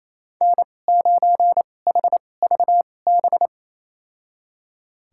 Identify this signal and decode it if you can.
Morse code, heard as a keyed tone, about 28 wpm, spaced as for 20 wpm, 700 Hz. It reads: N9HVB
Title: USA Call Signs - 5 Characters - 2x Character Spacing - 28wpm